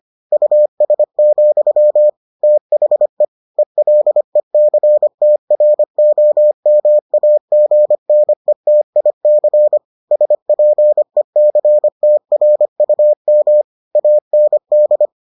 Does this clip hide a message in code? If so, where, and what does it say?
Yes: Morse code from 0.3 to 15.1 s, heard as a keyed tone, about 25 wpm, 600 Hz. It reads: US, THE ELECTROMAGNETIC SPECTRUM AND